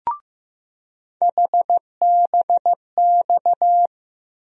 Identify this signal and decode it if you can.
Morse code, heard as a keyed tone, about 15 wpm, 700 Hz. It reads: HBX
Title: USA Call Sign Postfixes - 3 Letters 15wpm